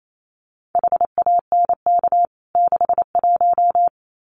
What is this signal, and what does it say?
Morse code, heard as a keyed tone, about 28 wpm, 700 Hz. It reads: HANK 61
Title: CWT Exchange 28wpm